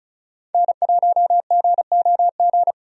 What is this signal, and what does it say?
Morse code, heard as a keyed tone, about 36 wpm, 700 Hz. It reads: N1GOG
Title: USA Call Signs - 5 Characters - 35wpm